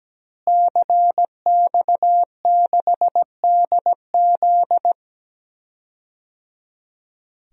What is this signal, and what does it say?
Morse code, heard as a keyed tone, about 17 wpm, 700 Hz. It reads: CX6DZ
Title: DXCC-100 Call Signs - 5 Characters - Repeated 3x - 17wpm